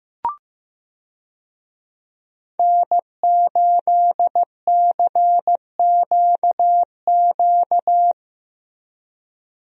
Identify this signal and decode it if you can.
Morse code, heard as a keyed tone, about 15 wpm, 700 Hz. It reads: N8CQQ
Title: USA Call Signs - 5 Characters - Repeated 3x - 15wpm